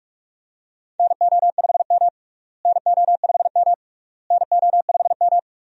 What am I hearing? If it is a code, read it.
Morse code, heard as a keyed tone, about 42 wpm, 700 Hz. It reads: NO5M NO5M NO5M
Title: USA Call Signs - 4 Characters - Repeated 3x - 45wpm